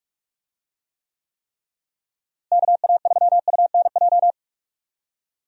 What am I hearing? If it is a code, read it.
Morse code, heard as a keyed tone, about 42 wpm, 700 Hz. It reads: KA3UNJ
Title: USA Call Signs - 6 Characters - Repeated 3x - 45wpm